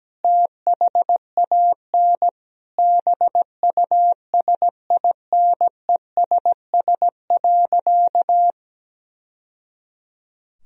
Morse code, heard as a keyed tone, about 17 wpm, 700 Hz. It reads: THAN BUSINESS.
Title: Sentences from Top 1000 Words - Review Part 2 - 17wpm